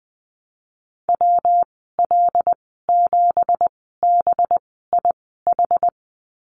Morse code, heard as a keyed tone, about 20 wpm, spaced as for 15 wpm, 700 Hz. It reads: WL7BIH
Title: 193 Lesson 31g - Letter v - Call Signs 20wpm